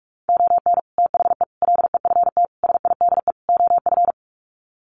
Morse code, heard as a keyed tone, about 44 wpm, 700 Hz. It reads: ON THE LEFT SIDE OF
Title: Sets of 5 Words Encore2 45wpm